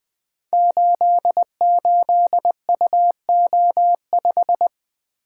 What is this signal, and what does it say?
Morse code, heard as a keyed tone, about 20 wpm, 700 Hz. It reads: 88UO5
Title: Code Groups - 5 Letter:Numbers - 20wpm